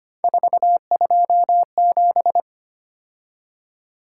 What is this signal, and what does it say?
Morse code, heard as a keyed tone, about 29 wpm, 700 Hz. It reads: 427
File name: Speed-Racing - Three Letter-Number Combinations - 25wpm